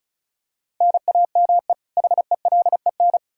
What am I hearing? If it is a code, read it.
Morse code, heard as a keyed tone, about 34 wpm, 700 Hz. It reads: NAME HELEN
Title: Practice QSO - Exchange 1 - Beginning - 35wpm